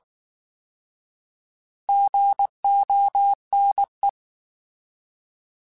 Morse code, heard as a keyed tone, about 19 wpm, 800 Hz. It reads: GONE